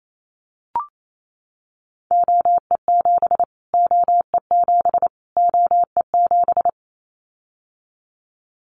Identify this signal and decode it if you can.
Morse code, heard as a keyed tone, about 28 wpm, 700 Hz. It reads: OE7 OE7 OE7
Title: Callsign Prefixes for Top 25 DXCC - Repeated 3x - 28wpm